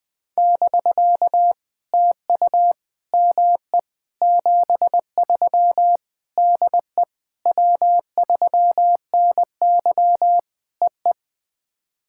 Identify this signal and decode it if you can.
Morse code, heard as a keyed tone, about 20 wpm, 700 Hz. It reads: <BK> TU ME 73 DE W3NY EE
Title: POTA QSO Practice - 20wpm